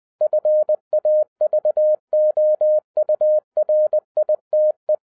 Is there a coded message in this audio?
Yes: Morse code from 0.2 to 4.9 s, heard as a keyed tone, about 20 wpm, 610 Hz. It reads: FAVOURITE